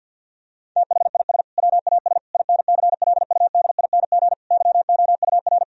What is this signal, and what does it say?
Morse code, heard as a keyed tone, about 50 wpm, 700 Hz. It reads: THIS WAS INCLUDING YOUR
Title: Sentences from Top 1000 Words - Review Part 4 - 50wpm